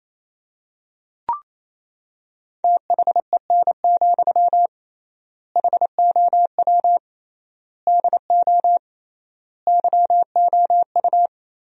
Morse code, heard as a keyed tone, about 28 wpm, 700 Hz. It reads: THEN, HOW DO YOU
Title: 3x Word-Spacing - Sentences from Top 100 Words - 28wpm